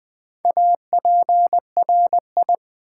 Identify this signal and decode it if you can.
Morse code, heard as a keyed tone, about 20 wpm, 700 Hz. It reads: APRI
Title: Build A Word - 1 Letter At A Time - 5-Letter Words - 20wpm